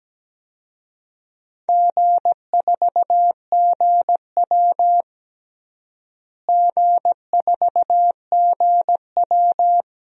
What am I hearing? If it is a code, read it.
Morse code, heard as a keyed tone, about 17 wpm, 700 Hz. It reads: G4GW G4GW
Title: DXCC-100 Call Signs - 4 Characters - Repeated 3x - 17wpm